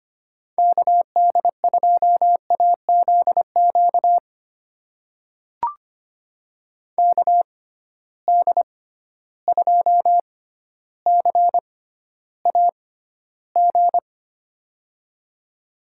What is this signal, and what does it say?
Morse code, heard as a keyed tone, about 25 wpm, spaced as for 13 wpm, 700 Hz. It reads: KD2AZQ KD2CAG
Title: USA Call Signs - 6 Characters - 6x Character Spacing - 25wpm